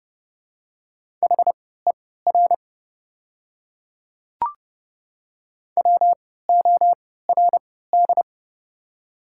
Morse code, heard as a keyed tone, about 30 wpm, spaced as for 17 wpm, 700 Hz. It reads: HER WORD
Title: 3x Character Spacing - Top 100 Words - 30wpm